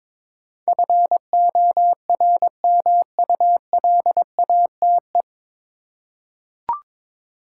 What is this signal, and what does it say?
Morse code, heard as a keyed tone, about 22 wpm, 700 Hz. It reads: FORMULATE
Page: Morse Code Ninja practice 22wpm